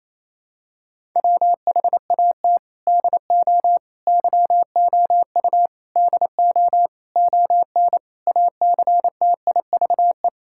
Morse code, heard as a keyed tone, about 28 wpm, 700 Hz. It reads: WHAT DO YOU DO ON ACTIVE
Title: Sentences from Top 1000 Words - Review Part 6 - 28wpm